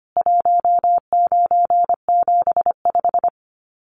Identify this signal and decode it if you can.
Morse code, heard as a keyed tone, about 25 wpm, 700 Hz. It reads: 1975
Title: Common QSO Phrases 25wpm